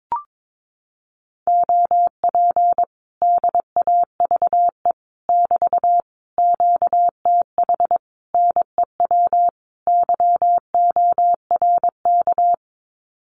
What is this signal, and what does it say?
Morse code, heard as a keyed tone, about 22 wpm, 700 Hz. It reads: OP DAVE <BT> QTH NEW YORK
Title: Common QSO Phrases Encore 22wpm